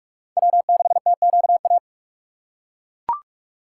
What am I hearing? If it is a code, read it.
Morse code, heard as a keyed tone, about 42 wpm, 690 Hz. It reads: W6TQA
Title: USA Call Signs - 5 Characters - 6x Character Spacing - 45wpm